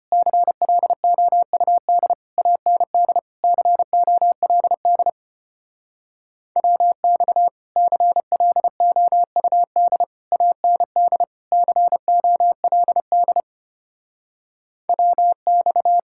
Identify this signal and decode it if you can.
Morse code, heard as a keyed tone, about 33 wpm, 700 Hz. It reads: CLOUD AND COLD WX CLOUD AND COLD WX